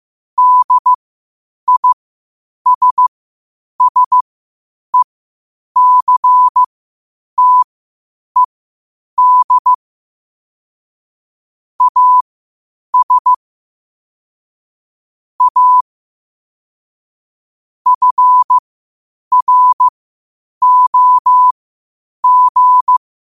Morse code, heard as a keyed tone, about 15 wpm, spaced as for 8 wpm, 1000 Hz. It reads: DISSECTED AS A FROG
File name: qotdfiles / 8wpm